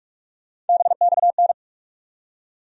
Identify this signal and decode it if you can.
Morse code, heard as a keyed tone, about 46 wpm, 700 Hz. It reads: BXN
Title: USA Call Sign Postfixes - 3 Letters 45wpm